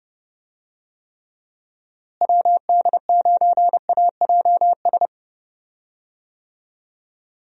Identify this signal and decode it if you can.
Morse code, heard as a keyed tone, about 30 wpm, 700 Hz. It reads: WD9AJS
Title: USA Call Signs - 6 Characters - Repeated 3x - 30wpm